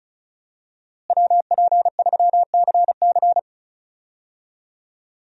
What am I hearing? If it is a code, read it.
Morse code, heard as a keyed tone, about 34 wpm, 700 Hz. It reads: WP3CC
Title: USA Call Signs - 5 Characters - Repeated 2x - 35wpm